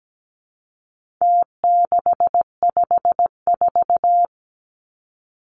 Morse code, heard as a keyed tone, about 17 wpm, 700 Hz. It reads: T654
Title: Numbers - Contest Style - 4-Digit 17wpm